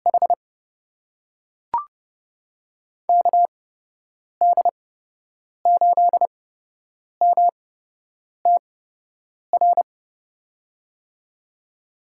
Morse code, heard as a keyed tone, about 30 wpm, spaced as for 8 wpm, 700 Hz. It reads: H KD8MTR
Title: USA Call Signs - 6 Characters - 8x Character Spacing - 30wpm